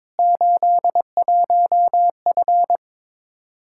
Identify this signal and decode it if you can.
Morse code, heard as a keyed tone, about 22 wpm, 700 Hz. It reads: 81F
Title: Practice QSO - Exchange 2 - Info - 22wpm